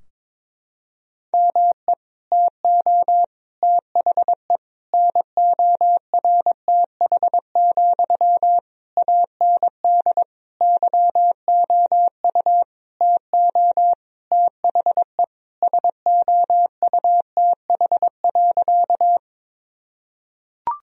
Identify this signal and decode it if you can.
Morse code, heard as a keyed tone, about 22 wpm, 700 Hz. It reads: ME TO THE NORTH, AND YOU TO THE SOUTH.
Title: Sentences from Top 1000 Words - Review Part 2 - 22wpm